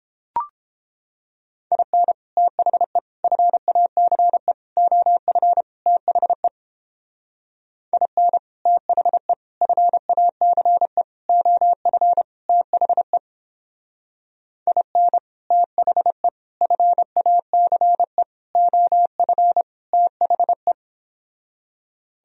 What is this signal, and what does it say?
Morse code, heard as a keyed tone, about 30 wpm, 700 Hz. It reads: IN THE FACE OF THE IN THE FACE OF THE IN THE FACE OF THE